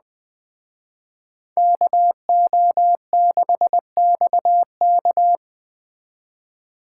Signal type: Morse code

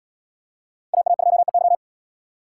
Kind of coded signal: Morse code

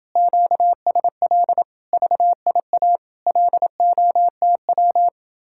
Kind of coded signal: Morse code

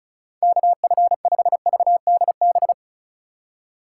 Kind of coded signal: Morse code

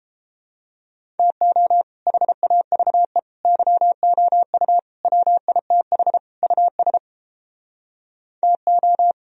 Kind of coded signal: Morse code

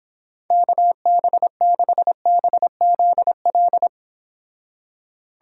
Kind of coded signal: Morse code

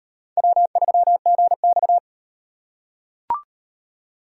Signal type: Morse code

